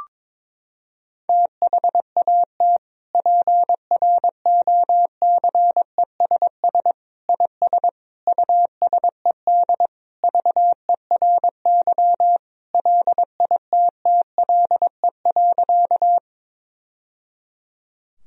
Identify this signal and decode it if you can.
Morse code, heard as a keyed tone, about 22 wpm, 700 Hz. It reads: THAT PROCESS IS USED VERY LITTLE.
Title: Sentences from Top 1000 Words - Review Part 3 - 22wpm